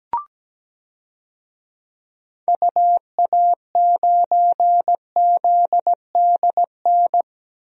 Morse code, heard as a keyed tone, about 17 wpm, 700 Hz. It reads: UA9ZDN
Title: DXCC-100 Call Signs - 6 Characters - 17wpm